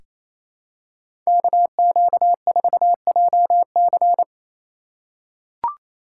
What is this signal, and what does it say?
Morse code, heard as a keyed tone, about 28 wpm, 700 Hz. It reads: KQ4JC
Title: USA Call Signs - 5 Characters - 4x Character Spacing - 28wpm